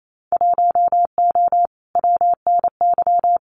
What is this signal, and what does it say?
Morse code, heard as a keyed tone, about 28 wpm, 700 Hz. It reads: 1O WNY
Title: Winter Field Day Exchanges Encore 28wpm